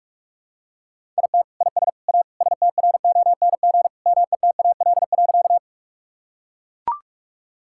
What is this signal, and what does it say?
Morse code, heard as a keyed tone, about 43 wpm, 700 Hz. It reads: IT IS A STRONG METAL.
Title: Sentences from Top 1000 Words - Review Part 6 - 45wpm